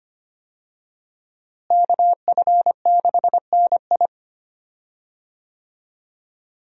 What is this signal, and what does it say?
Morse code, heard as a keyed tone, about 25 wpm, 700 Hz. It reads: KF6NI
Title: USA Call Signs - 5 Characters - Repeated 3x - 25wpm